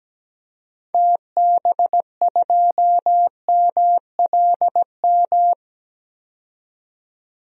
Morse code, heard as a keyed tone, about 17 wpm, 700 Hz. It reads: TB2MLM
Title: DXCC-100 Call Signs - 6 Characters - 17wpm